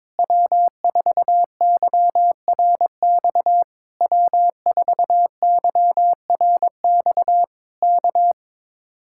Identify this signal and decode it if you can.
Morse code, heard as a keyed tone, about 22 wpm, 700 Hz. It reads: W4YRX W4YRX K
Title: Speed-Racing Common QSO Phrases - 22wpm